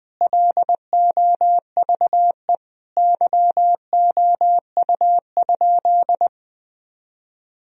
Morse code, heard as a keyed tone, about 20 wpm, 700 Hz. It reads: LOVE YOU?
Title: Sentences from Top 1000 Words - Review Part 3 - 20wpm